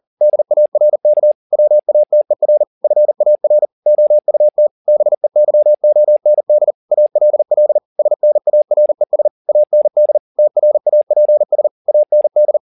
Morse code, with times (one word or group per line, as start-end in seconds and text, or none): 0.2-1.3 DARK
1.5-2.6 WATER
2.8-3.6 FAR
3.9-4.7 OUT
4.9-6.7 BEYOND
6.9-7.8 ALL
8.0-9.3 SNARES
9.5-10.2 AND
10.4-11.7 TRAPS
11.9-12.6 AND